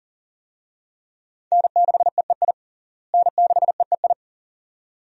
Morse code, heard as a keyed tone, about 40 wpm, 700 Hz. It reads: N6EEI N6EEI
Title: USA Call Signs - 5 Characters - Repeated 2x - 40wpm